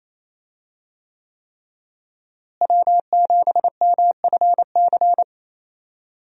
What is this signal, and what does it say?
Morse code, heard as a keyed tone, about 28 wpm, 700 Hz. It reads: W7MFC
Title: USA Call Signs - 5 Characters - Repeated 3x - 28wpm